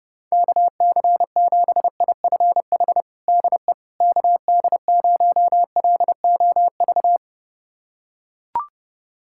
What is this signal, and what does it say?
Morse code, heard as a keyed tone, about 30 wpm, 700 Hz. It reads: KC7IFH DE KD0LOV
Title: USA Callsign DE Callsign 30wpm